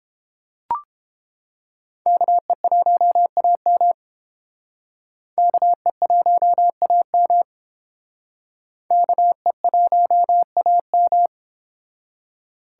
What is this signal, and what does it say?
Morse code, heard as a keyed tone, about 30 wpm, 700 Hz. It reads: KE1AM KE1AM KE1AM